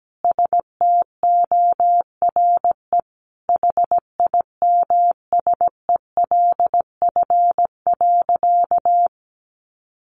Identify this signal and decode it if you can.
Morse code, heard as a keyed tone, about 17 wpm, 700 Hz. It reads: STORE HIMSELF.
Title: Easy Phrases 17wpm